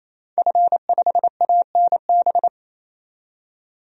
Morse code, heard as a keyed tone, about 28 wpm, 700 Hz. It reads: F5ANB
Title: DXCC-100 Call Signs - 5 Characters - 28wpm